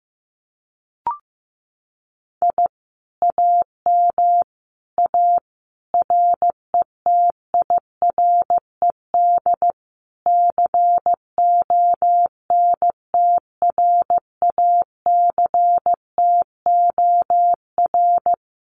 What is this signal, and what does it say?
Morse code, heard as a keyed tone, about 15 wpm, 700 Hz. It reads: I AM A RETIRED CONTRACTOR
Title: Common QSO Phrases Encore 15wpm